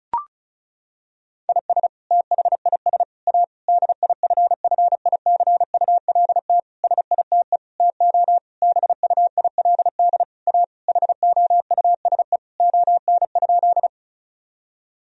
Morse code, heard as a keyed tone, about 36 wpm, 700 Hz. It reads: IS THIS A DIFFICULT SITE TO BUILD A HOUSE ON?
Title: Sentences from Top 1000 Words - Review Part 4 - 35wpm